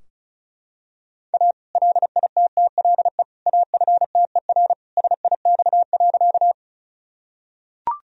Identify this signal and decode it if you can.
Morse code, heard as a keyed tone, about 36 wpm, 700 Hz. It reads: A LITTLE AFTER SIX.
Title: Sentences from Top 1000 Words - Review Part 2 - 35wpm